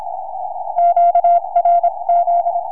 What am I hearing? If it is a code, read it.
Morse code, heard as a keyed tone, about 27 wpm, spaced as for 23 wpm, 720 Hz. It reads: QRZ
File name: sv1iw 4m